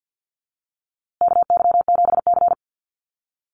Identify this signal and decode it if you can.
Morse code, heard as a keyed tone, about 50 wpm, 700 Hz. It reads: KY7C
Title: USA Call Signs Repeated 3x 50wpm